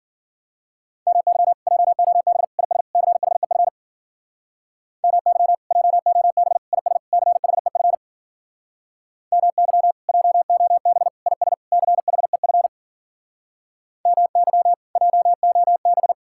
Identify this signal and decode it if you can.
Morse code, heard as a keyed tone, about 52 wpm, 700 Hz. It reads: MY JOB IS CHEF MY JOB IS CHEF MY JOB IS CHEF MY JOB